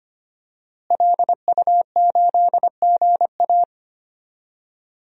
Morse code, heard as a keyed tone, about 25 wpm, 700 Hz. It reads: LU8GA
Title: DXCC-100 Call Signs - 5 Characters - Repeated 2x - 25wpm